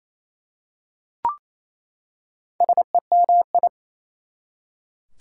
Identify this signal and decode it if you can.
Morse code, heard as a keyed tone, about 28 wpm, 700 Hz. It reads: SEMI
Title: Prefixes and Suffixes 28wpm